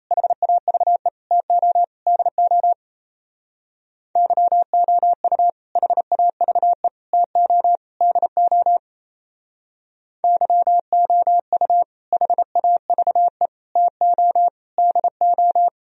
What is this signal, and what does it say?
Morse code, heard as a keyed tone, about 35 wpm, 700 Hz. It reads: HAVE TO DO YOU HAVE TO DO YOU HAVE TO DO
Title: Speed-Racing - Sets of 4 Words 28wpm